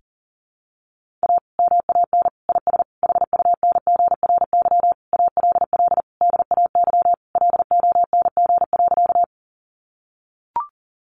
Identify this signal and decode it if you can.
Morse code, heard as a keyed tone, about 40 wpm, 700 Hz. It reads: A MAN IS HUNGRY ALL DAY LONG.
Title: Sentences from Top 1400 Words 40wpm